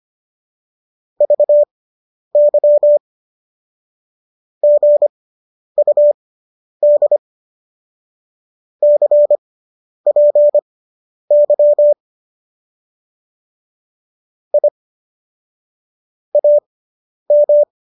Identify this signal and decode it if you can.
Morse code, heard as a keyed tone, about 25 wpm, spaced as for 10 wpm, 590 Hz. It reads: VY GUD CPY I AM